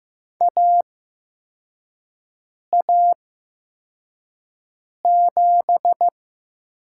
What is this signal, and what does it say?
Morse code, heard as a keyed tone, about 15 wpm, spaced as for 4 wpm, 700 Hz. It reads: AA7
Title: USA Call Signs - 4 Characters - 8x Character Spacing - 15wpm